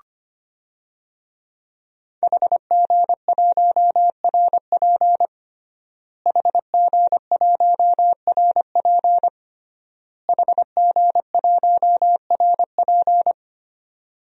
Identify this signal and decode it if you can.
Morse code, heard as a keyed tone, about 25 wpm, 700 Hz. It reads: HG1RP HG1RP HG1RP
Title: DXCC-100 Call Signs - 5 Characters - Repeated 3x - 25wpm